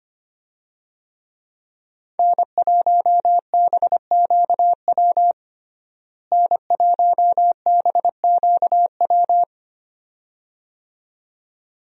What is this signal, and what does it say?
Morse code, heard as a keyed tone, about 25 wpm, 700 Hz. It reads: N1BQW N1BQW
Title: USA Call Signs - 5 Characters - Repeated 2x - 25wpm